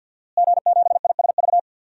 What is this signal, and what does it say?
Morse code, heard as a keyed tone, about 51 wpm, 700 Hz. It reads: G7ISV
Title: DXCC-100 Call Signs - 5 Characters - Repeated 2x - 50wpm